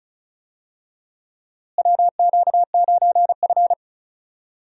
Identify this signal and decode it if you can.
Morse code, heard as a keyed tone, about 36 wpm, 700 Hz. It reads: WQ9F
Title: USA Call Signs - 4 Characters - Repeated 2x - 35wpm